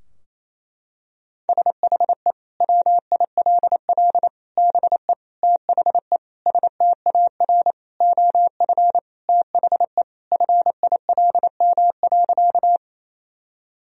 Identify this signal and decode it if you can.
Morse code, heard as a keyed tone, about 28 wpm, 700 Hz. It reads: SHE WILL BE THE STAR OF THE FILM.
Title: Sentences from Top 1000 Words - Review Part 3 - 28wpm